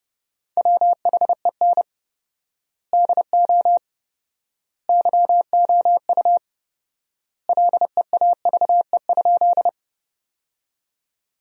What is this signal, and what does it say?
Morse code, heard as a keyed tone, about 30 wpm, 700 Hz. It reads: WHEN DO YOU LEAVE?
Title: 4x Word-Spacing - Sentences from Top 400 Words - 30wpm